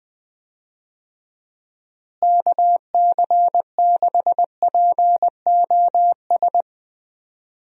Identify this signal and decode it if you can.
Morse code, heard as a keyed tone, about 20 wpm, 700 Hz. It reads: KC6POS
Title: USA Call Signs - 6 Characters - Repeated 2x - 20wpm